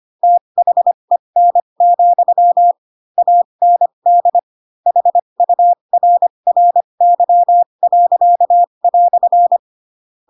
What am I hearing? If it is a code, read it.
Morse code, heard as a keyed tone, about 25 wpm, spaced as for 22 wpm, 700 Hz. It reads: THEN, AND HURRY."
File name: Pinocchio / Pinocchio ch4 18 25 ik1hge